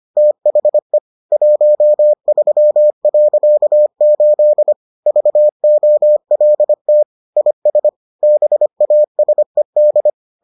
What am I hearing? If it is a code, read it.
Morse code, heard as a keyed tone, about 25 wpm, 600 Hz. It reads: THE 13.8 VOLT IS BASED